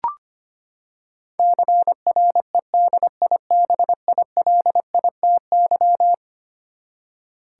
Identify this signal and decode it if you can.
Morse code, heard as a keyed tone, about 25 wpm, 700 Hz. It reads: CREDIBILITY
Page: morse-code-ninja-practice-25wpm